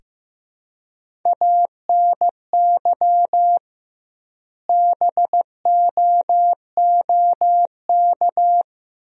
Morse code, heard as a keyed tone, about 15 wpm, 700 Hz. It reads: ANY BOOK
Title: 2x Word-Spacing - Sentences from Top 300 Words - 15wpm